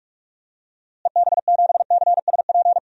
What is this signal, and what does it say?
Morse code, heard as a keyed tone, about 45 wpm, 700 Hz. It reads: EB7CSP
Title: DXCC-100 Call Signs - 6 Characters - Repeated 3x - 45wpm